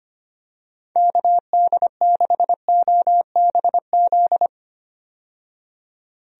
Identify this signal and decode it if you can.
Morse code, heard as a keyed tone, about 25 wpm, 700 Hz. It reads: KD6OBZ
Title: USA Call Signs - 6 Characters - 25wpm